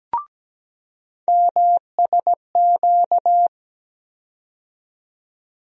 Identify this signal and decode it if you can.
Morse code, heard as a keyed tone, about 17 wpm, 700 Hz. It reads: MSQ
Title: USA Call Sign Postfixes - 3 Letters 17wpm